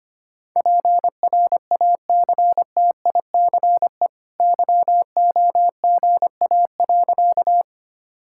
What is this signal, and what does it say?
Morse code, heard as a keyed tone, about 25 wpm, 690 Hz. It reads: PRACTICE YOGA.
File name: Easy Sentences with Hints - 25wpm